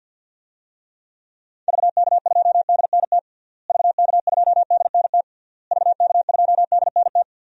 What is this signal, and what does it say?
Morse code, heard as a keyed tone, about 52 wpm, 700 Hz. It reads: VK2DNT VK2DNT VK2DNT
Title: DXCC-100 Call Signs - 6 Characters - Repeated 3x - 50wpm